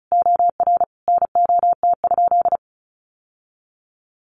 Morse code, heard as a keyed tone, about 34 wpm, 700 Hz. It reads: OR NOT?
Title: Sentences from Top 1000 Words 35wpm